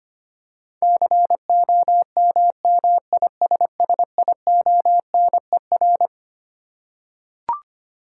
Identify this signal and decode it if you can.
Morse code, heard as a keyed tone, about 25 wpm, 700 Hz. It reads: COMMISSIONER
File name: Twelve Letter Words - 25wpm